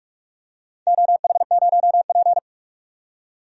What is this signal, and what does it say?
Morse code, heard as a keyed tone, about 42 wpm, 700 Hz. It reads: OH0P
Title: DXCC-100 Call Signs - 4 Characters - 45wpm